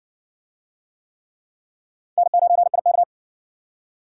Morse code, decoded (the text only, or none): N9IK